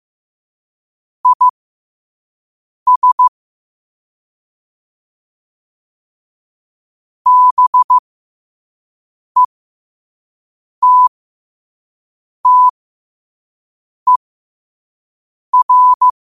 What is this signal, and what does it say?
Morse code, heard as a keyed tone, about 15 wpm, spaced as for 5 wpm, 1000 Hz. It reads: IS BETTER
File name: qotdfiles / 5wpm